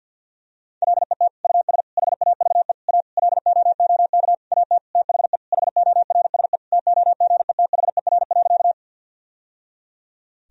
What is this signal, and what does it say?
Morse code, heard as a keyed tone, about 49 wpm, 700 Hz. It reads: LET US HAVE A LOOK AT THE HORSE TOGETHER.
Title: Sentences from Top 1000 Words - Review Part 1 - 50wpm